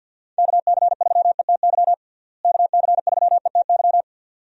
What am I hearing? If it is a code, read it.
Morse code, heard as a keyed tone, about 51 wpm, 700 Hz. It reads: KC3ETY KC3ETY
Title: USA Call Signs - 6 Characters - Repeated 2x - 50wpm